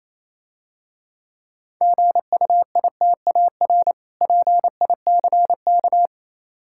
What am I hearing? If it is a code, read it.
Morse code, heard as a keyed tone, about 27 wpm, 700 Hz. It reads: GUITAR PICK
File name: Easy Words - Speed-Racing with Hints - 22wpm